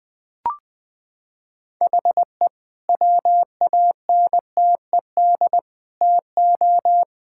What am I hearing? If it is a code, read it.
Morse code, heard as a keyed tone, about 21 wpm, 700 Hz. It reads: HE WANTED TO
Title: Sentences with 3-Word Phrases - 20wpm